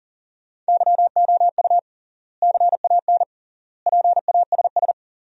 Morse code, heard as a keyed tone, about 40 wpm, 700 Hz. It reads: YOU CAN PASS
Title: 3x Word-Spacing - Sentences from Top 400 Words - 40wpm